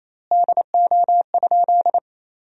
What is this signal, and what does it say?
Morse code, heard as a keyed tone, about 28 wpm, 700 Hz. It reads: DO?
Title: 4x Word-Spacing - Sentences from Top 100 Words - 28wpm